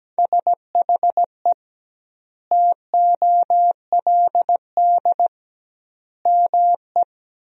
Morse code, heard as a keyed tone, about 17 wpm, 700 Hz. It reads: SHE TOLD ME
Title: 2x Word-Spacing - Sentences from Top 400 Words - 17wpm